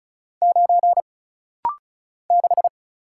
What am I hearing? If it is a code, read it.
Morse code, heard as a keyed tone, about 36 wpm, 700 Hz. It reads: 9 6
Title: Single Letter-Number - Rapid-Fire 35wpm